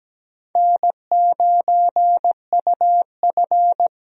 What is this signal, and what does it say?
Morse code, heard as a keyed tone, about 17 wpm, 700 Hz. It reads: N9UF
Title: USA Call Signs - 4 Characters - 17wpm